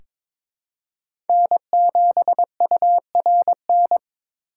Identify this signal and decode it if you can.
Morse code, heard as a keyed tone, about 22 wpm, 700 Hz. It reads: N7URN
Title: USA Call Signs - 5 Characters - 2x Character Spacing - 22wpm